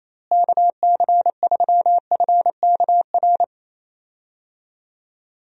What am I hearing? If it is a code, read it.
Morse code, heard as a keyed tone, about 28 wpm, 700 Hz. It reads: KC3FKR
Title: USA Call Signs - 6 Characters - Repeated 2x - 28wpm